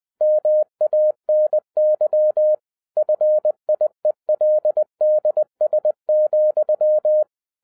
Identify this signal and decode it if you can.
Morse code, heard as a keyed tone, about 20 wpm, 610 Hz. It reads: MANY FIELDS,
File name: 2011-10-mp3 / n13-TIPS FROM MY 34 YEARS OF EXPERIENCE 20wpm